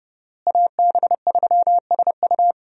Morse code, heard as a keyed tone, about 32 wpm, 700 Hz. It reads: AB3SU
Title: USA Call Signs - 5 Characters - Repeated 3x - 30wpm